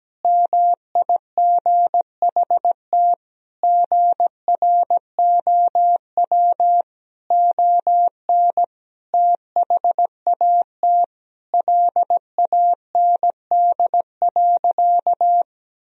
Morse code, heard as a keyed tone, about 17 wpm, 700 Hz. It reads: MIGHT GROW ON THAT LAND.
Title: Sentences from Top 1000 Words - Review Part 1 - 17wpm